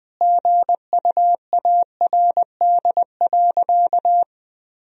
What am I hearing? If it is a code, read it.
Morse code, heard as a keyed tone, about 21 wpm, 700 Hz. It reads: GUARD.
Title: Sentences from Top 1000 Words - Review Part 4 - 20wpm